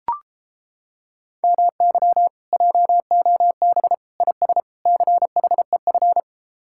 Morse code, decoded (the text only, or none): MY JOB IS CHEF